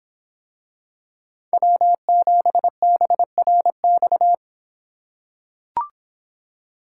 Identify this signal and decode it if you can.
Morse code, heard as a keyed tone, about 25 wpm, 700 Hz. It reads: W7BRX